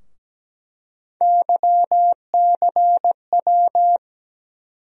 Morse code, heard as a keyed tone, about 17 wpm, 700 Hz. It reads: YCW
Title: USA Call Sign Postfixes - 3 Letters 17wpm